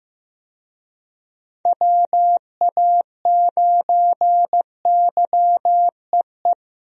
Morse code, heard as a keyed tone, about 15 wpm, 700 Hz. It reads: WA9YEE
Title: USA Call Signs - 6 Characters - Repeated 3x - 15wpm